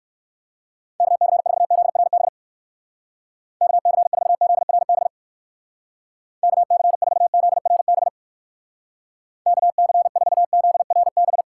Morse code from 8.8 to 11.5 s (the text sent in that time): KC4ZRB